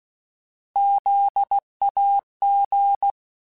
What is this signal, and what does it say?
Morse code, heard as a keyed tone, about 16 wpm, 800 Hz. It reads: ZAG
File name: Alphabetical / Alphabetical mp3s / Zs @15